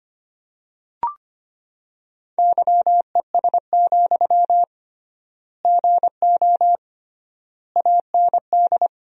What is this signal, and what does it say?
Morse code, heard as a keyed tone, about 25 wpm, 700 Hz. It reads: YES, GO AND
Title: 3x Word-Spacing - Sentences from Top 500 Words - 25wpm